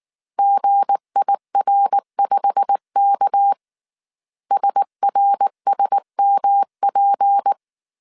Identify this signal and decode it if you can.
Morse code, heard as a keyed tone, about 19 wpm, 800 Hz. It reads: GIR5K SRSMP
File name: CW / DET / 1 9 1500C S A 18wpm 2459